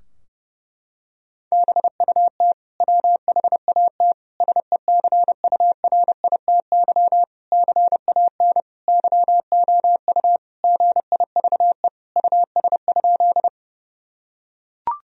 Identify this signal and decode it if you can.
Morse code, heard as a keyed tone, about 32 wpm, 700 Hz. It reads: BUT WHAT SECURITY CAN YOU GIVE US?
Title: Sentences from Top 1000 Words - Review Part 4 - 30wpm